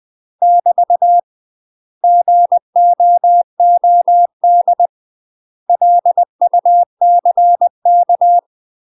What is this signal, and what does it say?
Morse code, heard as a keyed tone, about 20 wpm, 700 Hz. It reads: <BT> GOOD LUCK
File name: UK MASTER QSOs TRAINING / 20WPM / UK MASTER QSOs 0068